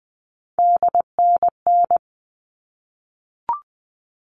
Morse code, heard as a keyed tone, about 20 wpm, 700 Hz. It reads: DNN
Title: All Cut Numbers - 3-Digit 20wpm